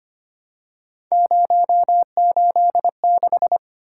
Morse code, heard as a keyed tone, about 25 wpm, 700 Hz. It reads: 086
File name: Three Letter-Number Combinations - 25wpm